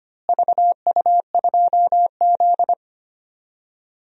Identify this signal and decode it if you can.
Morse code, heard as a keyed tone, about 25 wpm, 700 Hz. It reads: VU2Z
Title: DXCC-100 Call Signs - 4 Characters - Repeated 3x - 25wpm